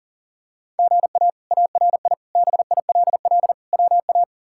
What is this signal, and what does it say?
Morse code, heard as a keyed tone, about 40 wpm, 700 Hz. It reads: GA ARI BILL WA
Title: K1USN SST QSO Practice - 40wpm